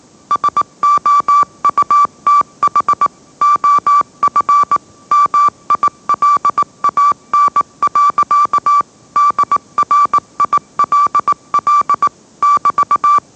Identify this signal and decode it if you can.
Morse code, heard as a keyed tone, about 18 wpm, 1160 Hz. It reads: SOUTH OF MILAN. DRILL <BT>